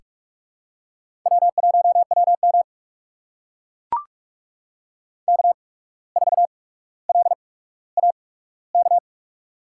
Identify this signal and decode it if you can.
Morse code, heard as a keyed tone, about 44 wpm, spaced as for 18 wpm, 700 Hz. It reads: W1WM K4LAK
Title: USA Call Signs - 8x Character Spacing - 45wpm